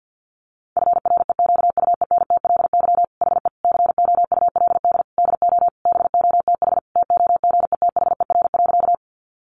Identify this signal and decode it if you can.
Morse code, heard as a keyed tone, about 50 wpm, 700 Hz. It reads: FREQUENTLY HE COULD DO BOTH TOGETHER.
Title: Sentences from Top 1300 Words 50wpm